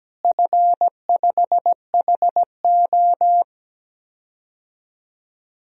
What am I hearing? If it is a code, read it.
Morse code, heard as a keyed tone, about 17 wpm, 700 Hz. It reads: F5HO
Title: DXCC-100 Call Signs - 4 Characters - Repeated 3x - 17wpm